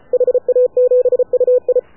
Morse code, heard as a keyed tone, about 33 wpm, 500 Hz. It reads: HA7UI
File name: AUDIO2019 / HA7UI CW